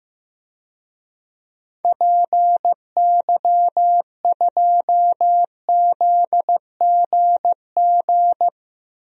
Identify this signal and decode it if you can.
Morse code, heard as a keyed tone, about 16 wpm, 700 Hz. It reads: PY2ZGG
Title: DXCC-100 Call Signs - 6 Characters - Repeated 2x - 15wpm